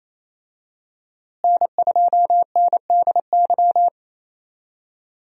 Morse code, heard as a keyed tone, about 28 wpm, 700 Hz. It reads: N2NDY
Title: USA Call Signs - 5 Characters - Repeated 2x - 28wpm